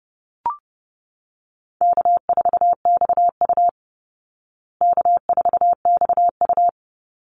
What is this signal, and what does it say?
Morse code, heard as a keyed tone, about 30 wpm, 700 Hz. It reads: K4XU K4XU
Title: Easy Call Signs Repeated 2x - 30wpm